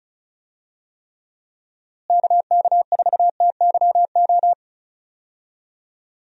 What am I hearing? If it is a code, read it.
Morse code, heard as a keyed tone, about 36 wpm, 700 Hz. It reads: KK4TYO
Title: USA Call Signs - 6 Characters - Repeated 3x - 35wpm